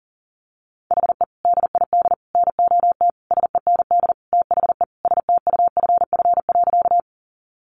Morse code, heard as a keyed tone, about 40 wpm, 700 Hz. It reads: HE DID NOT SEND THE STUFF.
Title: Sentences from Top 1500 Words 40wpm